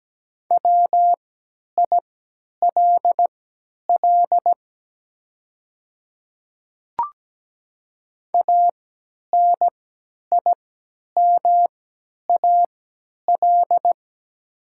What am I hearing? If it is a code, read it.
Morse code, heard as a keyed tone, about 17 wpm, spaced as for 10 wpm, 700 Hz. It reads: WILL ANIMAL
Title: 3x Character Spacing - Top 200 Words - 17wpm